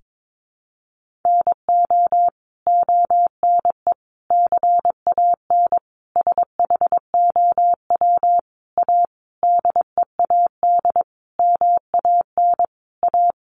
Morse code, heard as a keyed tone, about 22 wpm, 700 Hz. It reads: NO ONE CAN SHOW A DEAD MAN A
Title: Sentences from Top 1500 Words 22wpm